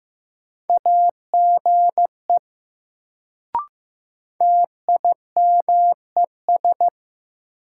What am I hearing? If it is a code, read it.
Morse code, heard as a keyed tone, about 15 wpm, 700 Hz. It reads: AGE TIMES
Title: Sets of 3 Words - New Word Drill - 15wpm